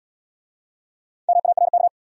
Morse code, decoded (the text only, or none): MARK